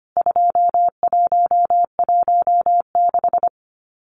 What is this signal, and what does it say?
Morse code, heard as a keyed tone, about 25 wpm, 690 Hz. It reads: 2116
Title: CWT Exchange 25wpm